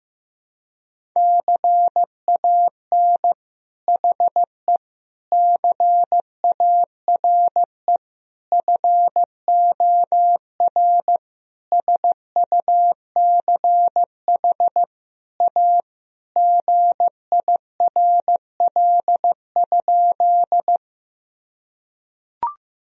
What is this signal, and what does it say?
Morse code, heard as a keyed tone, about 15 wpm, 700 Hz. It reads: CAN HE CARE FOR SUCH A GIRL?
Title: Sentences from Top 1000 Words - Review Part 1 - 15wpm